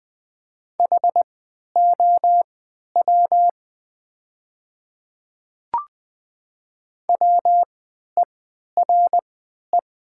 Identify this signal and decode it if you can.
Morse code, heard as a keyed tone, about 20 wpm, spaced as for 11 wpm, 700 Hz. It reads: HOW WERE
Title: 3x Character Spacing - Top 100 Words - 20wpm